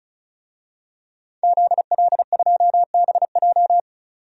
Morse code, heard as a keyed tone, about 36 wpm, 700 Hz. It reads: ZL2BJ
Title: DXCC-100 Call Signs - 5 Characters - 35wpm